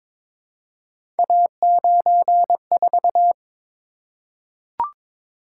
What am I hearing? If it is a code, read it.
Morse code, heard as a keyed tone, about 22 wpm, 700 Hz. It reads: A94
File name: practice / 22wpm / Three Letter-Number Combinations - 22wpm